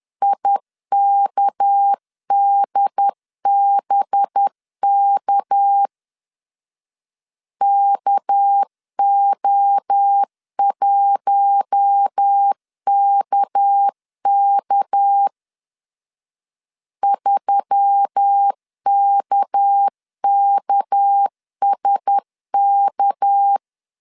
Morse code, heard as a keyed tone, about 10 wpm, 800 Hz. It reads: IKDBK KO1KK 3KKSK
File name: LEZIONI / 6 PKX0 / 6 PK 10wpm 2436